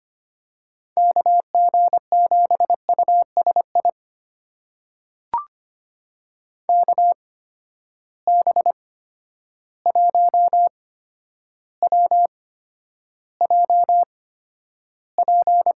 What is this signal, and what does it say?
Morse code, heard as a keyed tone, about 25 wpm, spaced as for 11 wpm, 700 Hz. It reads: KG7USI KB1WJP
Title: USA Call Signs - 6 Characters - 8x Character Spacing - 25wpm